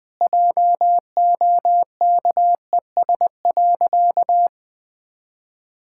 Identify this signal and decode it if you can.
Morse code, heard as a keyed tone, about 20 wpm, 700 Hz. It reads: JOKES.
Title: Pangram Challenge v2 - 20wpm